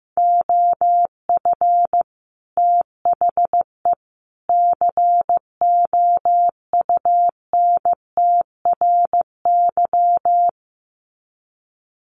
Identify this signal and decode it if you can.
Morse code, heard as a keyed tone, about 15 wpm, 700 Hz. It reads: OF THE COUNTRY
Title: Sets of 3 Words 15wpm